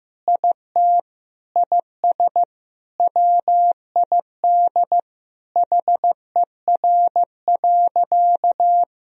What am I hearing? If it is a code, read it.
Morse code, heard as a keyed tone, about 15 wpm, 700 Hz. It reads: IT IS WID HER.
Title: Sentences from Top 100 Words and Abbreviations 15wpm